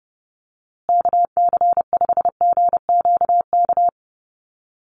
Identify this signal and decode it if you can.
Morse code, heard as a keyed tone, about 30 wpm, 700 Hz. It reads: KC5GQK
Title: USA Call Signs Repeated 2x 30wpm